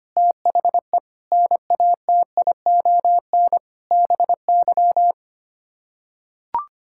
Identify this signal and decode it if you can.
Morse code, heard as a keyed tone, about 25 wpm, 700 Hz. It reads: THE NATION BY
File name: Sets of 9 Words - 25wpm